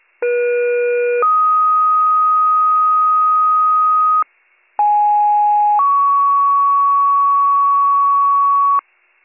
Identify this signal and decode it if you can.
Two-tone page, heard as a one-second tone, then a three-second tone, about 480 then 1250 Hz, about 830 then 1120 Hz. It is heard from 0.2 to 8.8 s.